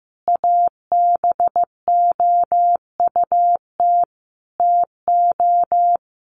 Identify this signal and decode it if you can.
Morse code, heard as a keyed tone, about 15 wpm, 700 Hz. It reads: ABOUT TO
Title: Sentences from Top 1500 Words 15wpm